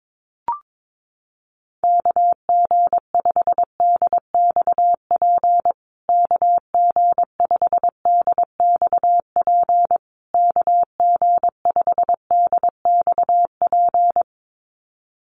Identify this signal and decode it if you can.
Morse code, heard as a keyed tone, about 22 wpm, 700 Hz. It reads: KG5DXP KG5DXP KG5DXP
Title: USA Call Signs Repeated 3x 22wpm